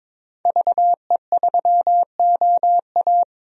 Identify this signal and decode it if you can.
Morse code, heard as a keyed tone, about 22 wpm, 700 Hz. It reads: VE3OA
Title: DXCC-100 Call Signs - 5 Characters - 22wpm